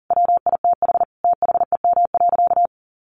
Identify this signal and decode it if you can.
Morse code, heard as a keyed tone, about 40 wpm, 700 Hz. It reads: WITH THEM.
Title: Sentences from Top 1400 Words 40wpm